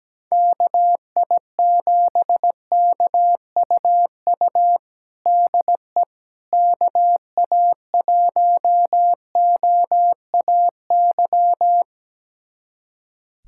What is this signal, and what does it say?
Morse code, heard as a keyed tone, about 17 wpm, 700 Hz. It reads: KI7KUU DE KA1OAY
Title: USA Callsign DE Callsign 17wpm